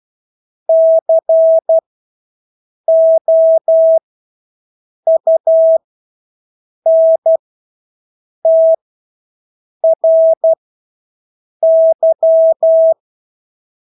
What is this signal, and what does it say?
Morse code, heard as a keyed tone, about 12 wpm, spaced as for 6 wpm, 650 Hz. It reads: COUNTRY
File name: Engelske ord 12-6 wpm